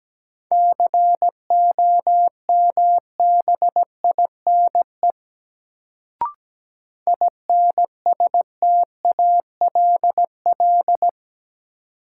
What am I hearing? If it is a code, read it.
Morse code, heard as a keyed tone, about 17 wpm, 700 Hz. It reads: COMBINE INSTALL
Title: Sentences from Top 1600 Words - New Word Drill - 17wpm